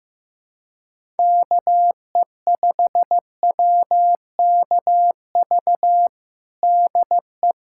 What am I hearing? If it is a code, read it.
Morse code, heard as a keyed tone, about 16 wpm, 700 Hz. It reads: KE5WKV DE
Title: USA Callsign DE Callsign - 6 Characters - 15wpm